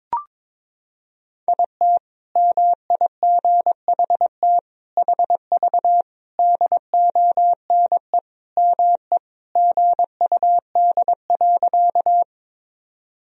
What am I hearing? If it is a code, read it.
Morse code, heard as a keyed tone, about 22 wpm, 700 Hz. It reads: IT MIGHT HV DONE ME GUD.
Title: Sentences from Top 500 Words and Abbreviations 22wpm